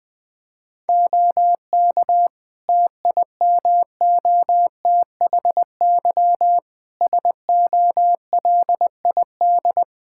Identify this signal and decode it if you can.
Morse code, heard as a keyed tone, about 20 wpm, 700 Hz. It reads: OK TIMOTHY SOLID
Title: K1USN SST QSO Practice 20wpm